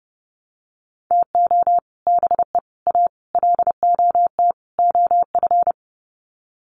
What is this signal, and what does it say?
Morse code, heard as a keyed tone, about 30 wpm, 700 Hz. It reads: TO BE A LOT OF
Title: Sets of 5 Words Encore 30wpm